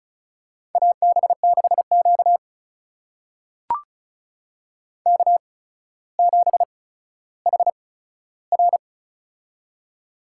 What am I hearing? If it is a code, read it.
Morse code, heard as a keyed tone, about 36 wpm, spaced as for 15 wpm, 700 Hz. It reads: AB6Q K7HR
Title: USA Call Signs - 8x Character Spacing - 35wpm